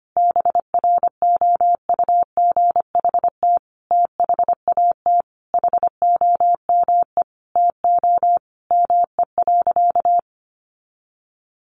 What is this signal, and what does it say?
Morse code, heard as a keyed tone, about 25 wpm, 700 Hz. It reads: BROUGHT THAT HOME TO ME.
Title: Sentences from Top 1100 Words 25wpm